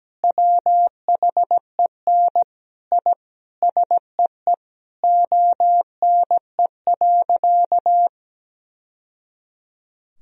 Morse code, heard as a keyed tone, about 17 wpm, 700 Hz. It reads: WHEN I SEE ONE.
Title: Sentences from Top 1000 Words - Review Part 3 - 17wpm